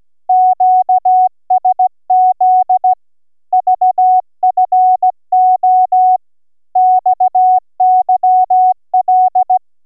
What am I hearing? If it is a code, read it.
Morse code, heard as a keyed tone, about 16 wpm, 740 Hz. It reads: QSZ VFO XYL